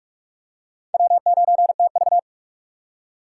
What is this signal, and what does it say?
Morse code, heard as a keyed tone, about 44 wpm, 690 Hz. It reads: W9TV
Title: USA Call Signs - 4 Characters - 2x Character Spacing - 45wpm